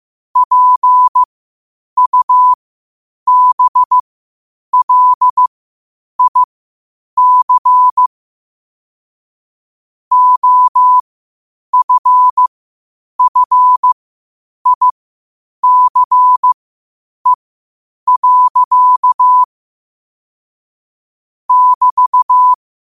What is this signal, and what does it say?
Morse code, heard as a keyed tone, about 15 wpm, spaced as for 8 wpm, 1000 Hz. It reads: PUBLIC OFFICE. <BT>
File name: qotdfiles / 8wpm